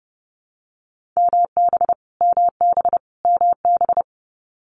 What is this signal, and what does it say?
Morse code, heard as a keyed tone, about 30 wpm, 700 Hz. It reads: MB MB MB
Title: Canadian Provinces and Territories - Repeated 3x - 30wpm